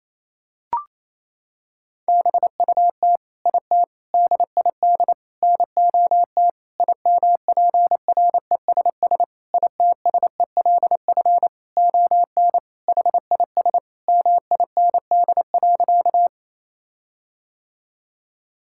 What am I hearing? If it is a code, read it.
Morse code, heard as a keyed tone, about 28 wpm, 700 Hz. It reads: BUT IT DID NOT IMPRESS ITSELF ON HIS MIND.
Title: Sentences from Top 1600 Words 28wpm